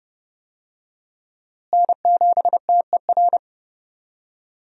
Morse code, heard as a keyed tone, about 31 wpm, 700 Hz. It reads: N7TER
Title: USA Call Signs - 5 Characters - Repeated 3x - 30wpm